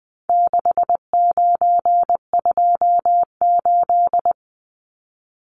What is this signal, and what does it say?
Morse code, heard as a keyed tone, about 20 wpm, 700 Hz. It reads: 6928
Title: Numbers - 4-Digit 20wpm